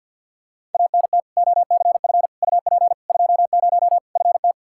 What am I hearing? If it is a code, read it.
Morse code, heard as a keyed tone, about 52 wpm, 700 Hz. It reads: ANT OCF UP 20 FT
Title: K1USN SST QSO Practice 50wpm